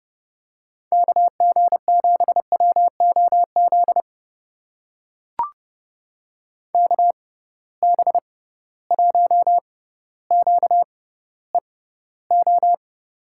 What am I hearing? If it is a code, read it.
Morse code, heard as a keyed tone, about 30 wpm, spaced as for 15 wpm, 700 Hz. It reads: KG7WOZ KB1QEO
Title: USA Call Signs - 6 Characters - 6x Character Spacing - 30wpm